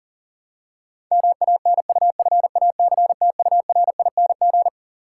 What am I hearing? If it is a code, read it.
Morse code, heard as a keyed tone, about 40 wpm, 700 Hz. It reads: MANUFACTURING
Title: Practice QSO - Exchange 3 - More Info - 40wpm